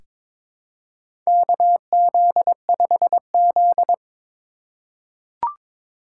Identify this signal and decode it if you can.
Morse code, heard as a keyed tone, about 21 wpm, 700 Hz. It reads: KZ5Z
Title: K1USN SST QSO Practice - 22wpm